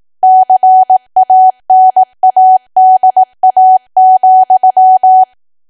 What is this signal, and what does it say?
Morse code, heard as a keyed tone, about 18 wpm, 750 Hz. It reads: CANADA,
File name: cwa / W1AW-Level2 / 18WPM